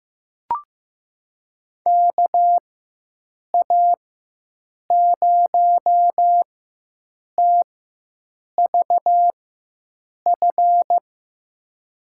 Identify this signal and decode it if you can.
Morse code, heard as a keyed tone, about 15 wpm, spaced as for 7 wpm, 700 Hz. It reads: KA0TVF
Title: USA Call Signs - 6 Characters - 4x Character Spacing - 15wpm